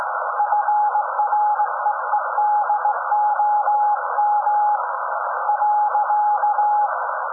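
Morse code, heard as a keyed tone, about 18 wpm, 840 Hz. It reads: FREEDOM .
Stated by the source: SAQ Grimeton Radio, Alexanderson 200 kW alternator on 17.2 kHz, received in Potsdam with a very strong signal on World Radio Day.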